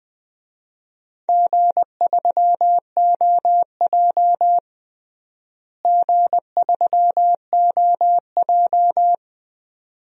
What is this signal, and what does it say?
Morse code, heard as a keyed tone, about 20 wpm, 700 Hz. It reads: G3OJ G3OJ
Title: DXCC-100 Call Signs - 4 Characters - Repeated 2x - 20wpm